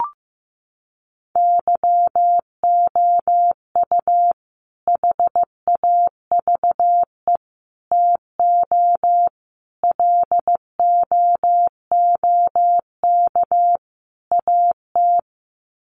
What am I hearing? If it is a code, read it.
Morse code, heard as a keyed tone, about 15 wpm, 700 Hz. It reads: YOU HAVE TO LOOK AT
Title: Sets of 5 Words Encore 15wpm